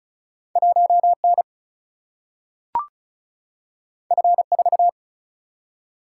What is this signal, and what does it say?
Morse code, heard as a keyed tone, about 36 wpm, 700 Hz. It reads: 1 N F 4
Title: Single-Letter Single-Number 35wpm